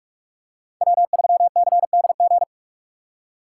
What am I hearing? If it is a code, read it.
Morse code, heard as a keyed tone, about 43 wpm, 690 Hz. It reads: W3CDG
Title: USA Call Signs - 5 Characters - 2x Character Spacing - 45wpm